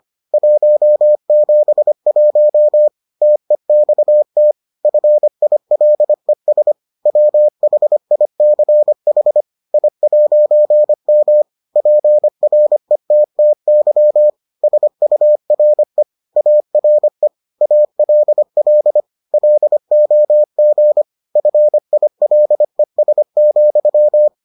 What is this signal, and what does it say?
Morse code, heard as a keyed tone, about 25 wpm, 600 Hz. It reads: 171 TEXT FILES WHICH I'M PRETTY SURE ARE ALL LOG FILES,